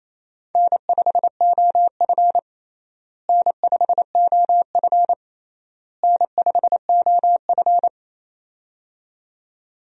Morse code, heard as a keyed tone, about 28 wpm, 700 Hz. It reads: N5OF N5OF N5OF
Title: USA Call Signs - 4 Characters - Repeated 3x - 28wpm